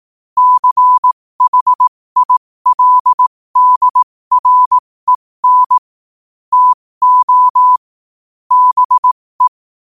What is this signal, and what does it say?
Morse code, heard as a keyed tone, about 18 wpm, spaced as for 15 wpm, 1000 Hz. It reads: CHILDREN TO BE